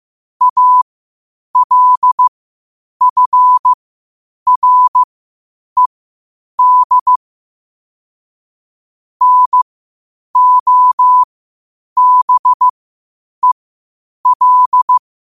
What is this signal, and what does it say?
Morse code, heard as a keyed tone, about 15 wpm, spaced as for 8 wpm, 1000 Hz. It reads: ALFRED NOBEL
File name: qotdfiles / 8wpm